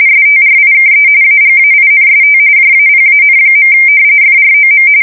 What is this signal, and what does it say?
RTTY, about 45.45 baud. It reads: STRIKE AGAINST JAPAN. FROM
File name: fsk170 / battleshipnj